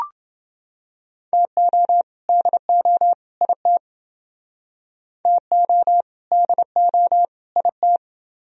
Morse code, heard as a keyed tone, about 28 wpm, 700 Hz. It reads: TO DO IT TO DO IT